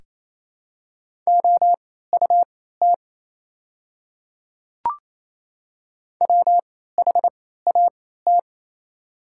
Morse code, heard as a keyed tone, about 28 wpm, spaced as for 16 wpm, 700 Hz. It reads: OUT WHAT
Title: 3x Character Spacing - Top 500 Words - 28wpm